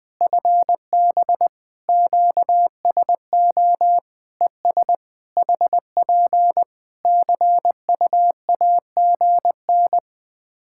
Morse code, heard as a keyed tone, about 19 wpm, 700 Hz. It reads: FB QSO ES HP CUAGN
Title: Practice QSO - Exchange 4 - Ending - 20wpm